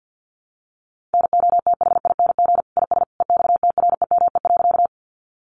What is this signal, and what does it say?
Morse code, heard as a keyed tone, about 50 wpm, 700 Hz. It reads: NOTHING IS EXTREME.
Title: Sentences from Top 1300 Words 50wpm